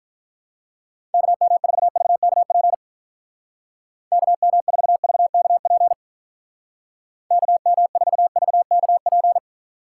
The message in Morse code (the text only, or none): KM4VKP KM4VKP KM4VKP